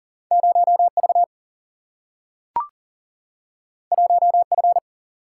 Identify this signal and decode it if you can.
Morse code, heard as a keyed tone, about 39 wpm, 700 Hz. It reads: 0 V 1 F
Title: Single-Letter Single-Number 40wpm